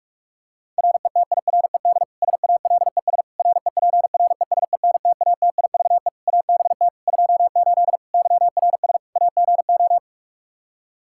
Morse code, heard as a keyed tone, about 46 wpm, 700 Hz. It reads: RETIRED SALES REPRESENTATIVE ABT 28 YRS AGO
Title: Practice QSO - Exchange 3 - More Info - 45wpm